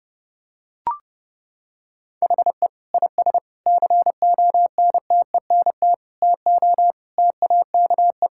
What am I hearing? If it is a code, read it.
Morse code, heard as a keyed tone, about 32 wpm, 700 Hz. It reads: HE IS CONTENT TO TAKE
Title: Sentences from Top 1000 Words - Review Part 6 - 30wpm